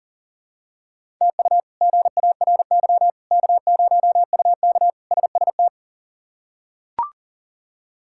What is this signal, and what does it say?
Morse code, heard as a keyed tone, about 40 wpm, 700 Hz. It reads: TU GARY K0UK SST
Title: K1USN SST QSO Practice - 40wpm